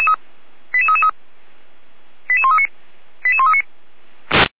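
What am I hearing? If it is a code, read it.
Five-tone selective call (ZVEI), heard as quick runs of five tones, tones of about 70 ms. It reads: 88333 88139 88139
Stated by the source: ZVEI five-tone selective call 88335 received by SDR, recorded by feuerwehr-orlamuende (Orlamünde fire brigade).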